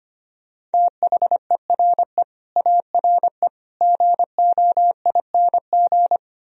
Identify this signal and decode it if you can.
Morse code, heard as a keyed tone, about 29 wpm, 700 Hz. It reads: THERE ARE GOING
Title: Speed-Racing - Sets of 5 Words 25wpm